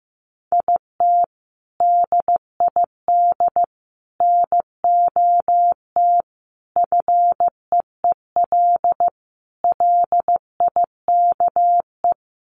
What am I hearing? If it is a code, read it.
Morse code, heard as a keyed tone, about 15 wpm, 700 Hz. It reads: IT DID NOT FEEL LIKE
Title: Sentences from Top 1100 Words 15wpm